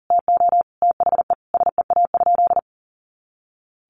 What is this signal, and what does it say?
Morse code, heard as a keyed tone, about 40 wpm, 700 Hz. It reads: TO THE SEA?
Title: Sentences from Top 300 Words 40wpm